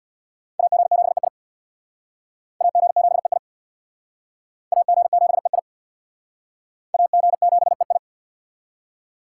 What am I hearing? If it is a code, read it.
Morse code, heard as a keyed tone, about 64 wpm, 700 Hz. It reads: AG7EI AG7EI AG7EI AG7EI